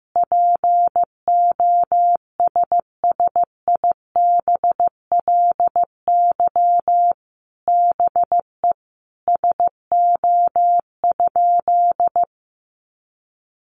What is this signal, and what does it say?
Morse code, heard as a keyed tone, about 15 wpm, 700 Hz. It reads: POSSIBLY BE SO?
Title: Sentences from Top 1300 Words 15wpm